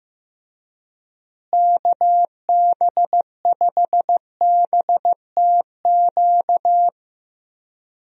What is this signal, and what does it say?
Morse code, heard as a keyed tone, about 15 wpm, 700 Hz. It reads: KB5BTQ
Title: USA Call Signs - 6 Characters - Repeated 3x - 15wpm